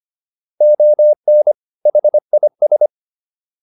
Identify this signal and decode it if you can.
Morse code, heard as a keyed tone, about 25 wpm, 600 Hz. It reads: ON HIS
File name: WD104 25